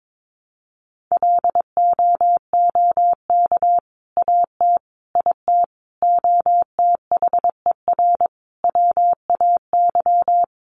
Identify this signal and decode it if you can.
Morse code, heard as a keyed tone, about 22 wpm, 700 Hz. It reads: LOOK AT IT OTHER WAY
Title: Sentences from Top 200 Words 22wpm